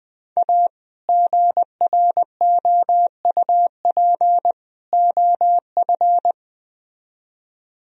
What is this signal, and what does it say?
Morse code, heard as a keyed tone, about 20 wpm, 700 Hz. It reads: A GROUP OF
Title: Speed-Racing - Sets of 3 Words 17wpm